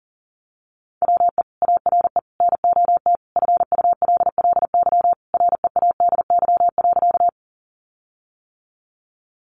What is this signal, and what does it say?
Morse code, heard as a keyed tone, about 40 wpm, 700 Hz. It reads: WE ARE NOT FULLY READY.
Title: Sentences from Top 1300 Words 40wpm